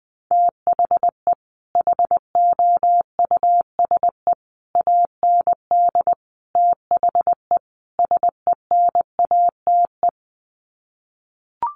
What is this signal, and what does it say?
Morse code, heard as a keyed tone, about 20 wpm, 700 Hz. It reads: THE HOUSE AND THE SENATE
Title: Sets of 5 Words Encore2 20wpm